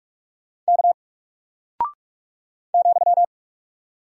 Morse code, heard as a keyed tone, about 46 wpm, 700 Hz. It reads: K ,
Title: Single Character (Letter:Number:Punctuation:Prosign) Alt 45wpm